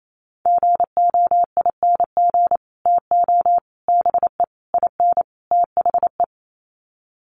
Morse code, heard as a keyed tone, about 28 wpm, 700 Hz. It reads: GOING TO BE IN THE
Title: Sets of 5 Words Encore 28wpm